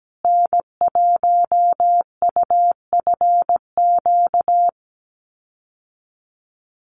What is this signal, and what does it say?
Morse code, heard as a keyed tone, about 17 wpm, 700 Hz. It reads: N1UFQ
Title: USA Call Signs Repeated 3x 17wpm